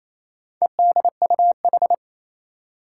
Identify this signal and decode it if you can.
Morse code, heard as a keyed tone, about 28 wpm, 700 Hz. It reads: EDUH
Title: Four Challenging Letter-Number Combinations 28wpm